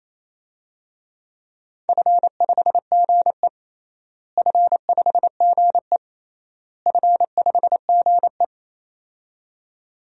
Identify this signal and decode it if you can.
Morse code, heard as a keyed tone, about 28 wpm, 700 Hz. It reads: F5GE F5GE F5GE
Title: DXCC-100 Call Signs - 4 Characters - Repeated 3x - 28wpm